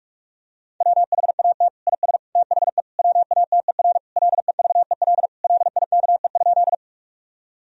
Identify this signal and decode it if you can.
Morse code, heard as a keyed tone, about 46 wpm, 700 Hz. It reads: WHAT IS THE WATER LEVEL LIKE?
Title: Sentences from Top 1000 Words - Review Part 4 - 45wpm